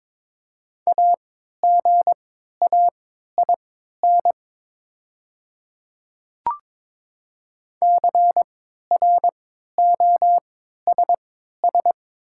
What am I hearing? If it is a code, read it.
Morse code, heard as a keyed tone, about 22 wpm, spaced as for 12 wpm, 700 Hz. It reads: AGAIN CROSS
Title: 3x Character Spacing - Top 300 Words - 22wpm